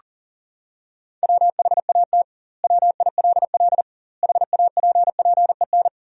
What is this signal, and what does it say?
Morse code, heard as a keyed tone, about 40 wpm, 700 Hz. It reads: WHAT WILL HAPPEN
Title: 2x Word-Spacing - Sentences from Top 400 Words - 40wpm